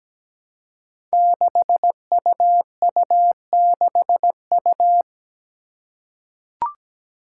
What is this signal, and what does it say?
Morse code, heard as a keyed tone, about 17 wpm, 700 Hz. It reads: 6UU6U
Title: Five Challenging Letter-Number Combinations 17wpm